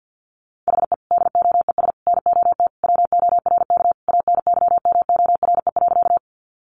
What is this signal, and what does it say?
Morse code, heard as a keyed tone, about 50 wpm, 700 Hz. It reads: HE DOES NOT WORK ANYMORE.
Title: Additional Easy Phrases 50wpm